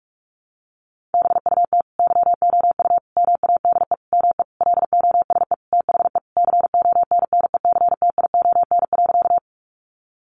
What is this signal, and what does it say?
Morse code, heard as a keyed tone, about 44 wpm, 700 Hz. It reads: BUT YOU MADE ME LOSE THE CONNECTION.
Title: Sentences from Top 1300 Words 45wpm